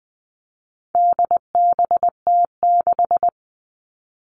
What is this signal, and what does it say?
Morse code, heard as a keyed tone, about 20 wpm, 700 Hz. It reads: DBT6
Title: All Cut Numbers - 4-Digit 20wpm